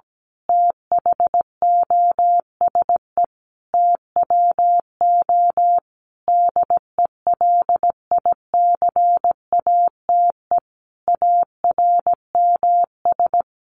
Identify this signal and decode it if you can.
Morse code, heard as a keyed tone, about 17 wpm, 700 Hz. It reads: THOSE TWO DELICATE ARMS
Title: A Sound Of Thunder 17wpm